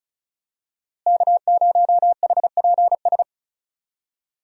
Morse code, heard as a keyed tone, about 36 wpm, 700 Hz. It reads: K0HPS
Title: USA Call Signs - 5 Characters - Repeated 3x - 35wpm